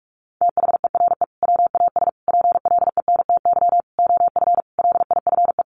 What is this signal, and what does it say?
Morse code, heard as a keyed tone, about 44 wpm, 700 Hz. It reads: THERE WAS PLENTY OF LIFE
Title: Sentences from Top 1300 Words 45wpm